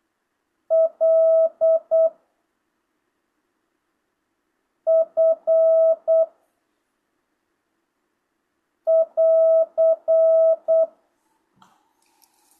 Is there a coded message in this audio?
Yes: Morse code from 0.7 to 10.8 s, heard as a keyed tone, about 8 wpm, 660 Hz. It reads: L F <AR>